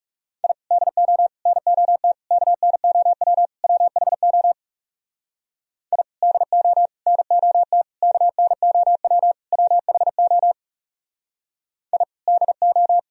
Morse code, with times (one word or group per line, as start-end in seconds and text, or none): 0.4-0.5 I
0.7-1.3 DO
1.5-2.1 NOT
2.3-3.5 KNOW
3.6-4.5 WHO
5.9-6.0 I
6.2-6.9 DO
7.1-7.8 NOT
8.0-9.3 KNOW
9.5-10.5 WHO
11.9-12.0 I
12.3-13.0 DO